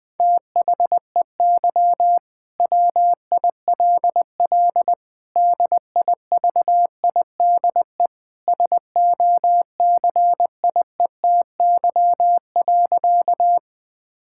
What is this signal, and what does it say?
Morse code, heard as a keyed tone, about 20 wpm, 700 Hz. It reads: THEY WILL DIVIDE SOCIETY.
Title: Sentences from Top 1600 Words 20wpm